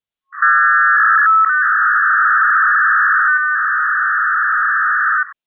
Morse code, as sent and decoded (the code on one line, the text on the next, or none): none